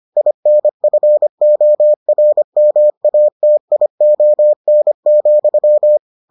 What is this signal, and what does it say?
Morse code, heard as a keyed tone, about 25 wpm, 600 Hz. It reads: INFORMATION,